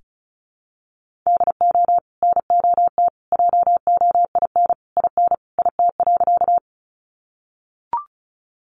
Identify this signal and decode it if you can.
Morse code, heard as a keyed tone, about 34 wpm, 700 Hz. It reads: DO NOT JOIN IN IT.
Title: Sentences from Top 1400 Words 35wpm